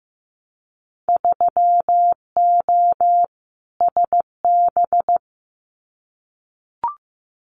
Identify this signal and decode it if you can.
Morse code, heard as a keyed tone, about 15 wpm, 700 Hz. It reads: 3O SB
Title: Winter Field Day Exchanges Encore 15wpm